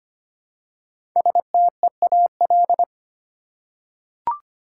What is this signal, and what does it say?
Morse code, heard as a keyed tone, about 25 wpm, 700 Hz. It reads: STEAL
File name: 5 Letter Words - First 3 Letters Given - 25wpm